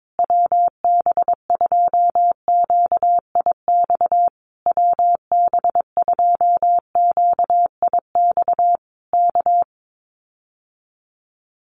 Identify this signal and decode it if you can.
Morse code, heard as a keyed tone, about 22 wpm, 700 Hz. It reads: WB2QIX WB2QIX K
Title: Common QSO Phrases Encore 22wpm